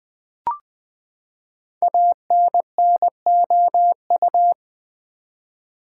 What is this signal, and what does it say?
Morse code, heard as a keyed tone, about 20 wpm, 700 Hz. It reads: ANNOU
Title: Build A Word - 5 Letters At A Time - 12-Letter Words - 20wpm